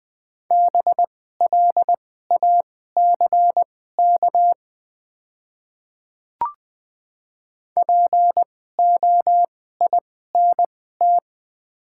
Morse code, heard as a keyed tone, about 20 wpm, spaced as for 15 wpm, 690 Hz. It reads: BLACK POINT
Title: 2x Character Spacing - Top 500 Words - 20wpm